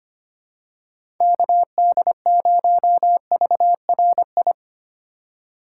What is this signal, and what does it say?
Morse code, heard as a keyed tone, about 25 wpm, 700 Hz. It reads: KD0VRI
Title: USA Call Signs - 6 Characters - 3x Character Spacing - 25wpm